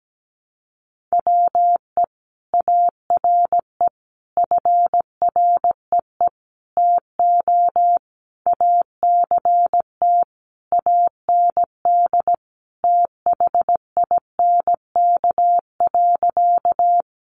Morse code, heard as a keyed tone, about 17 wpm, 700 Hz. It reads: WE ARE FREE TO ACT AND THINK.
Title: Sentences from Top 500 Words 17wpm